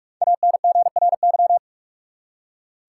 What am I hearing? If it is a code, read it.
Morse code, heard as a keyed tone, about 44 wpm, 700 Hz. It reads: ANGRY
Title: Five Letter Words - 45wpm